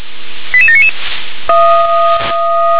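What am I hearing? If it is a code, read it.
Five-tone selective call (ZVEI), heard as quick runs of five tones, tones of about 70 ms. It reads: 88799
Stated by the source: ZVEI five-tone selective call received by SDR.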